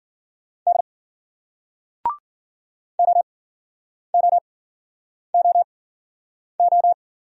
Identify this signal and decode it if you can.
Morse code, heard as a keyed tone, about 46 wpm, spaced as for 9 wpm, 700 Hz. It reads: D OOOO
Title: Speed-Racing Single Letters 40wpm